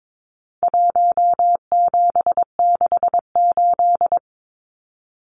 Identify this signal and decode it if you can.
Morse code, heard as a keyed tone, about 22 wpm, 700 Hz. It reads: 1768
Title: Numbers - Contest Style - 4-Digit 22wpm